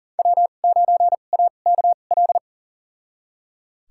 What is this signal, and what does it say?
Morse code, heard as a keyed tone, about 40 wpm, spaced as for 29 wpm, 700 Hz. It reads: W9AKL
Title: USA Call Signs - 5 Characters - 2x Character Spacing - 40wpm